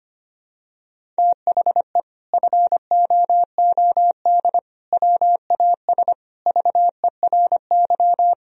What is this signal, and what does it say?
Morse code, heard as a keyed tone, about 25 wpm, 700 Hz. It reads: THE FOOD WAS VERY
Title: Sentences from Top 1000 Words - Review Part 1 - 25wpm